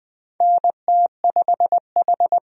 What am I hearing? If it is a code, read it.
Morse code, heard as a keyed tone, about 20 wpm, 700 Hz. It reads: NT5H
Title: USA Call Signs - 4 Characters - 20wpm